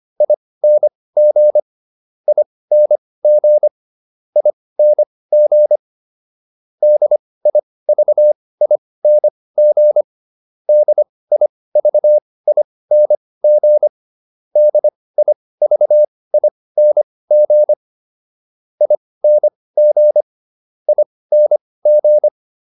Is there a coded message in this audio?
Yes: Morse code from 0.2 to 22.3 s, heard as a keyed tone, about 25 wpm, spaced as for 18 wpm, 600 Hz. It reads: ING ING ING DIVING DIVING DIVING ING ING